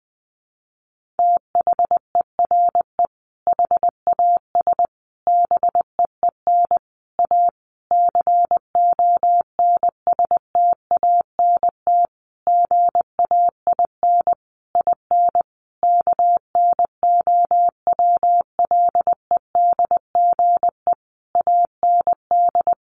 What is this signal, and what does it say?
Morse code, heard as a keyed tone, about 20 wpm, 700 Hz. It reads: THERE HAS BEEN A CONSTANT GAIN IN KNOWLEDGE AND
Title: Sentences from Top 1500 Words 20wpm